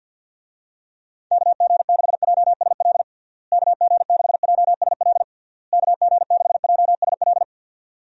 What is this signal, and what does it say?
Morse code, heard as a keyed tone, about 50 wpm, 700 Hz. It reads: KG6JSL KG6JSL KG6JSL
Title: USA Call Signs - 6 Characters - Repeated 3x - 50wpm